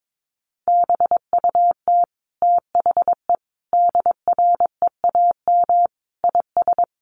Morse code, heard as a keyed tone, about 22 wpm, 700 Hz. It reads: BUT THE DREAM IS
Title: Sentences from Top 1500 Words 22wpm